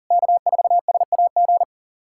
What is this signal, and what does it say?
Morse code, heard as a keyed tone, about 40 wpm, 700 Hz. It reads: K4SAG
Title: SOTA QSO Practice 40wpm